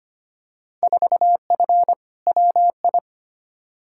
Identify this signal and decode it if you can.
Morse code, heard as a keyed tone, about 25 wpm, 700 Hz. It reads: 4F WI
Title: Field Day Exchanges - 25wpm